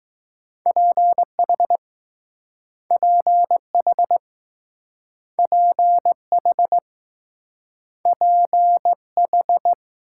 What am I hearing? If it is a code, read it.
Morse code, heard as a keyed tone, about 20 wpm, 700 Hz. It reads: PH PH PH PH